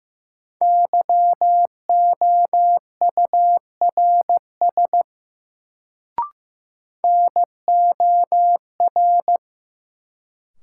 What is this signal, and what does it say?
Morse code, heard as a keyed tone, about 15 wpm, 700 Hz. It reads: YOURS NOR
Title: Sentences from Top 1500 Words - New Word Drill - 15wpm